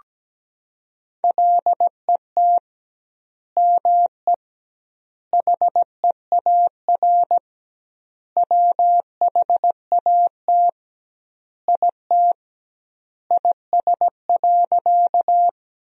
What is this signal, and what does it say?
Morse code, heard as a keyed tone, about 17 wpm, 700 Hz. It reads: LET ME HEAR WHAT IT IS.
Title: 2x Word-Spacing - Sentences from Top 300 Words - 17wpm